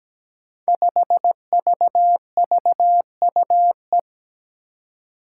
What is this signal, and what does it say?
Morse code, heard as a keyed tone, about 17 wpm, 700 Hz. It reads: 5VVUE
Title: Five Challenging Letter-Number Combinations 17wpm